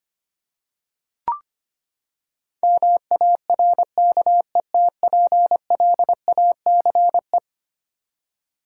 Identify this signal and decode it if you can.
Morse code, heard as a keyed tone, about 25 wpm, 700 Hz. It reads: MARKETPLACE
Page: morse-code-ninja-practice-25wpm